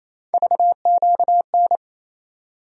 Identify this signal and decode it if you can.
Morse code, heard as a keyed tone, about 28 wpm, 700 Hz. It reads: VQN
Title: USA Call Sign Postfixes - 3 Letters 28wpm